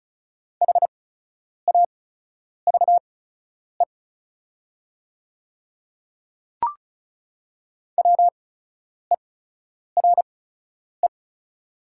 Morse code, heard as a keyed tone, about 36 wpm, spaced as for 10 wpm, 700 Hz. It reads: HAVE WERE
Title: 8x Character Spacing - Top 200 Words - 35wpm